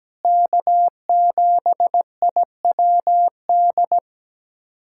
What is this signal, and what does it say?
Morse code, heard as a keyed tone, about 17 wpm, 700 Hz. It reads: K7IWD
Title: USA Callsign DE Callsign - 5 Characters - 17wpm